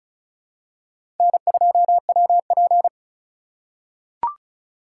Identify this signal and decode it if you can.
Morse code, heard as a keyed tone, about 36 wpm, 700 Hz. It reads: N2WP
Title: USA Call Signs - 4 Characters - 4x Character Spacing - 35wpm